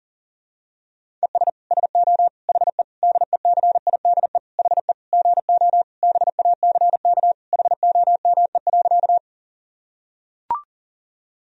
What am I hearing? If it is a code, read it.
Morse code, heard as a keyed tone, about 40 wpm, 700 Hz. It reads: ES SO HE DECIDE HE GO BACK HOME.
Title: Sentences from Top 500 Words and Abbreviations 40wpm